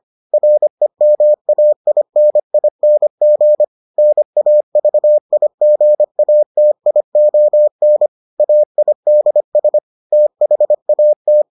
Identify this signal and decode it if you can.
Morse code, heard as a keyed tone, about 25 wpm, 600 Hz. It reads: REMAINING NAVIGATION AIDS THAT